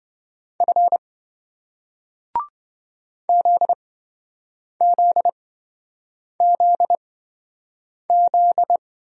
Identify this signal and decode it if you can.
Morse code, heard as a keyed tone, about 27 wpm, spaced as for 7 wpm, 700 Hz. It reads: F ZZZZ